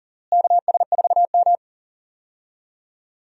Morse code, heard as a keyed tone, about 40 wpm, 700 Hz. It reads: KS4M
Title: USA Call Signs - 4 Characters - Repeated 3x - 40wpm